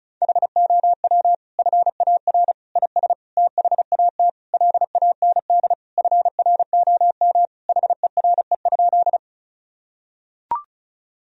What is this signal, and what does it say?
Morse code, heard as a keyed tone, about 36 wpm, 700 Hz. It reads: HOW FAR IS THAT LAND FROM HERE?
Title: Sentences from Top 1000 Words - Review Part 1 - 35wpm